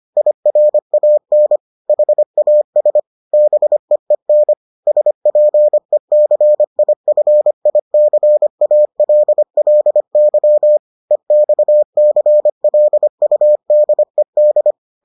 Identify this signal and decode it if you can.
Morse code, heard as a keyed tone, about 25 wpm, 600 Hz. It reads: IRAN HAS BEEN SPECIFICALLY EXCLUDED